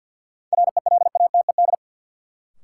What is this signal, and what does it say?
Morse code, heard as a keyed tone, about 52 wpm, 700 Hz. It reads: RELATED
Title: Sentences from Top 1400 Words - New Word Drill Alt - 50wpm